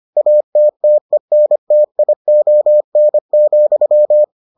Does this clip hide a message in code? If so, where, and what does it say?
Yes: Morse code from 0.2 to 4.2 s, heard as a keyed tone, about 25 wpm, 600 Hz. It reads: ATTENTION,